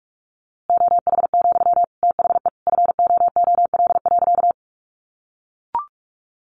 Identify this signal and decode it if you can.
Morse code, heard as a keyed tone, about 44 wpm, 700 Hz. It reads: OH, THE FOOL.
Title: A Sound Of Thunder 45wpm